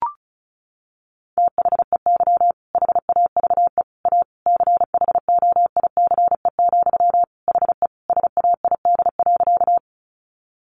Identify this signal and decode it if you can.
Morse code, heard as a keyed tone, about 34 wpm, 700 Hz. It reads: THEY HAVE A CHOICE, HE SAID.
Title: Sentences from Top 900 Words 35wpm